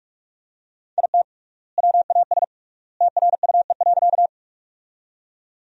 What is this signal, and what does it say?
Morse code, heard as a keyed tone, about 44 wpm, 700 Hz. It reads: IT WAS TRUE.
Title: 3x Word-Spacing - Sentences from Top 400 Words - 45wpm